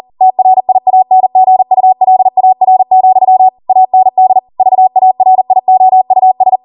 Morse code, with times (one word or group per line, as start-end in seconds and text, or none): 0.2-3.5 TRIANGULAR,
3.7-4.4 AND
4.6-6.6 VARIOUS